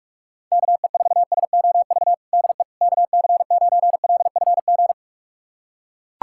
Morse code, heard as a keyed tone, about 41 wpm, 700 Hz. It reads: KE4SOV DE KC9LFG
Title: USA Callsign DE Callsign - 6 Characters - 45wpm